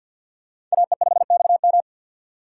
Morse code, heard as a keyed tone, about 49 wpm, 700 Hz. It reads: AE5XM
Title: USA Call Signs - 5 Characters - 2x Character Spacing - 50wpm